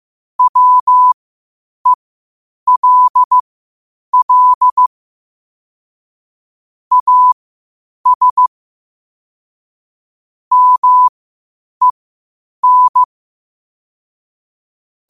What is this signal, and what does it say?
Morse code, heard as a keyed tone, about 15 wpm, spaced as for 8 wpm, 1000 Hz. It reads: WELL AS MEN
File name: qotdfiles / 8wpm